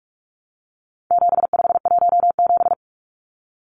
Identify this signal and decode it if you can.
Morse code, heard as a keyed tone, about 44 wpm, 700 Hz. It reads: 7517
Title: Numbers - 4-Digit 45wpm